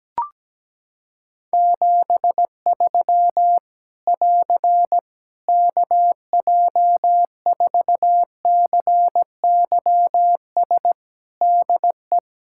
Morse code, heard as a keyed tone, about 17 wpm, 700 Hz. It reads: 73 <AR> KJ4CYS DE
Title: Practice QSO - Exchange 4 - Ending - 17wpm